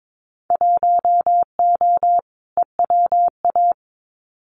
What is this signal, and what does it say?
Morse code, heard as a keyed tone, about 22 wpm, 700 Hz. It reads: 1O EWA
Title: Winter Field Day Exchanges Encore 22wpm